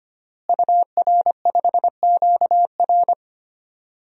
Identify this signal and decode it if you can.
Morse code, heard as a keyed tone, about 25 wpm, 700 Hz. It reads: UR5QR
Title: DXCC-100 Call Signs - 5 Characters - Repeated 3x - 25wpm